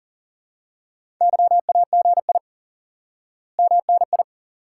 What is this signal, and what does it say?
Morse code, heard as a keyed tone, about 40 wpm, 700 Hz. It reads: YAGI MNI
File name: IQE Course 3 - Lesson 89 - YAGI - Two Elements - 40wpm